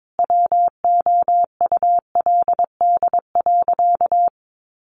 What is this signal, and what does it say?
Morse code, heard as a keyed tone, about 22 wpm, 700 Hz. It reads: WOULD.
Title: Sentences from Top 1500 Words 22wpm